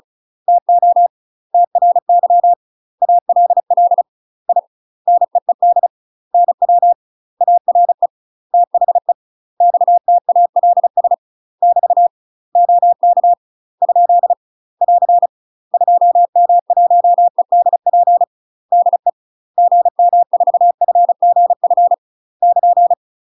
Morse code, heard as a keyed tone, about 36 wpm, 700 Hz. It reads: TO TRY ALL I NEED NW ARE THE XTALS <BT> OK ? <AR> 2M1EDP DE GM4FGF <KN>
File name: UK MASTER QSOs TRAINING / 35WPM / UK MASTER QSOs 0028